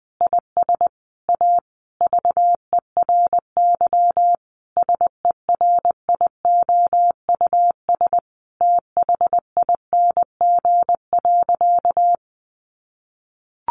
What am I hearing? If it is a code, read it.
Morse code, heard as a keyed tone, about 20 wpm, 700 Hz. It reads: IS A VERY SERIOUS THING.
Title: Sentences from Top 1500 Words 20wpm